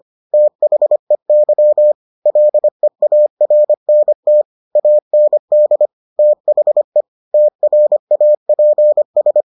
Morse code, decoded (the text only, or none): THEY LEARNT AND THE TRAPS